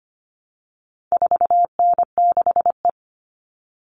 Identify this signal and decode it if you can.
Morse code, heard as a keyed tone, about 25 wpm, 700 Hz. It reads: 4N6E
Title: All Cut Numbers - 4-Digit 25wpm